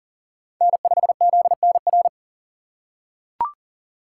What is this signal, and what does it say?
Morse code, heard as a keyed tone, about 40 wpm, 700 Hz. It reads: N5ZNR
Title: USA Call Signs - 5 Characters - 2x Character Spacing - 40wpm